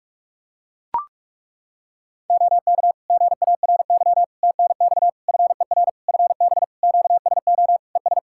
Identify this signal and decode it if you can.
Morse code, heard as a keyed tone, about 41 wpm, 700 Hz. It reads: OK GARY TNX FER FB QSO ES
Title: Practice QSO - Exchange 4 - Ending - 45wpm